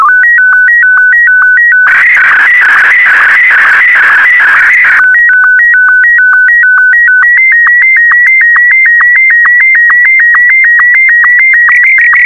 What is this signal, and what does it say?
SSTV (slow-scan television), Martin 1.
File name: sstv / M1-VK4MSL